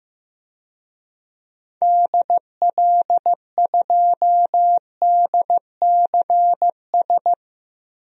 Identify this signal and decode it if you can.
Morse code, heard as a keyed tone, about 15 wpm, 700 Hz. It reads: DL2DCS
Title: DXCC-100 Call Signs - 6 Characters - Repeated 3x - 15wpm